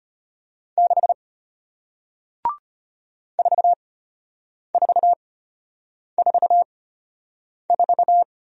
Morse code, heard as a keyed tone, about 33 wpm, spaced as for 8 wpm, 700 Hz. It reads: 6 4444